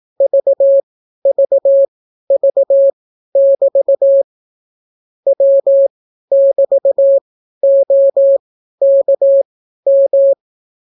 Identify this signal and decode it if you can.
Morse code, heard as a keyed tone, about 18 wpm, spaced as for 12 wpm, 550 Hz. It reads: VVV<BT> W<BT>OKM